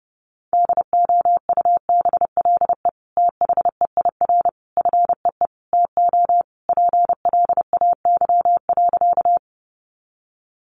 Morse code, heard as a keyed tone, about 30 wpm, 700 Hz. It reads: DOUBLE THEIR FEE TO PLAY.
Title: Sentences from Top 1500 Words 30wpm